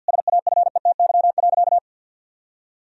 Morse code, heard as a keyed tone, about 50 wpm, 700 Hz. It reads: SAFETY.
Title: Sentences from Top 1000 Words - Review Part 4 - 50wpm